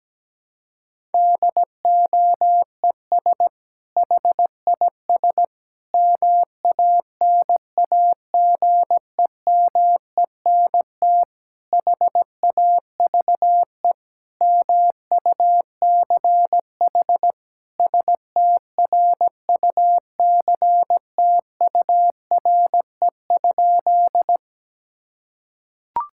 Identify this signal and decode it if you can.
Morse code, heard as a keyed tone, about 17 wpm, 700 Hz. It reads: DOES HIS MANAGEMENT HAVE MUCH STRUCTURE?
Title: Sentences from Top 1000 Words - Review Part 6 - 17wpm